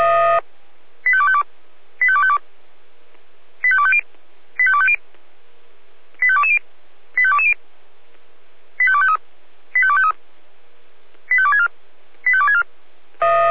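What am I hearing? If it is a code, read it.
Five-tone selective call (ZVEI), heard as quick runs of five tones, tones of about 70 ms. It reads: 85252 85252 85250 85250 85220 85220 85253 85253 85264 85264
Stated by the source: Received on SDR as ZVEI five-tone selective call, address 85280.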